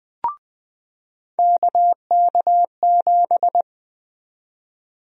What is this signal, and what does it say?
Morse code, heard as a keyed tone, about 20 wpm, 700 Hz. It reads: KK7
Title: USA Call Sign Prefixes - 20wpm